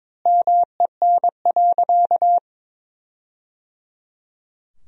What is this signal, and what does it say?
Morse code, heard as a keyed tone, about 22 wpm, 700 Hz. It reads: MEN.
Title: 2x Word-Spacing - Sentences from Top 300 Words - 22wpm